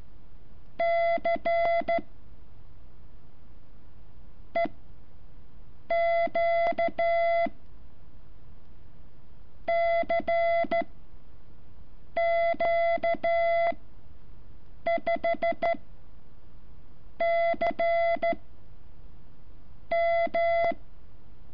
Morse code, heard as a keyed tone, about 13 wpm, spaced as for 6 wpm, 680 Hz. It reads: C EQ CQ5 C M
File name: curso / ejercicio 14